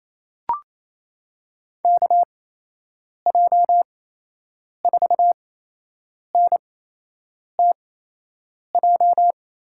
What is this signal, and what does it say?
Morse code, heard as a keyed tone, about 28 wpm, spaced as for 8 wpm, 700 Hz. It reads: KJ4NTJ
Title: USA Call Signs - 8x Character Spacing - 28wpm